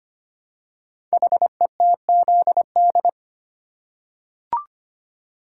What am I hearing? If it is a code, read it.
Morse code, heard as a keyed tone, about 25 wpm, 690 Hz. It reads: HETZD
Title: Code Groups - 5 Letters - 25wpm